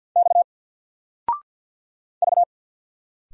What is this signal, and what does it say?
Morse code, heard as a keyed tone, about 50 wpm, 700 Hz. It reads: X V
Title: Single Letter-Number 50wpm